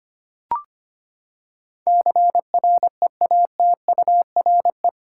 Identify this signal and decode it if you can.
Morse code, heard as a keyed tone, about 25 wpm, 700 Hz. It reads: CREATURE
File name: Eight Letter Words - 25wpm